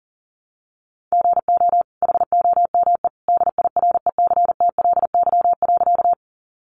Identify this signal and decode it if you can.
Morse code, heard as a keyed tone, about 40 wpm, 700 Hz. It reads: GO HOME DIRECTLY.
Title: Sentences from Top 800 Words 40wpm